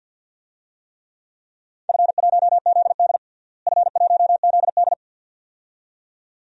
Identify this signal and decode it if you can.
Morse code, heard as a keyed tone, about 47 wpm, 700 Hz. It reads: F1ZD F1ZD
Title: DXCC-100 Call Signs - 4 Characters - Repeated 2x - 50wpm